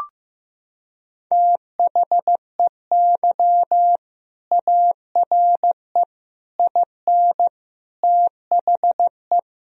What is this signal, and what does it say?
Morse code, heard as a keyed tone, about 15 wpm, 700 Hz. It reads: THEY ARE IN THE
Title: Sentences from Top 1000 Words - Review Part 3 - 15wpm